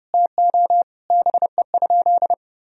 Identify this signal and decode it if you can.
Morse code, heard as a keyed tone, about 32 wpm, 700 Hz. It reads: TO BE?
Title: Sentences from Top 1000 Words - Review Part 2 - 30wpm